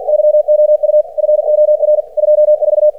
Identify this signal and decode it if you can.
Morse code, heard as a keyed tone, about 50 wpm, 600 Hz. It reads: YOU FOR 24
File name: archived / CRSnet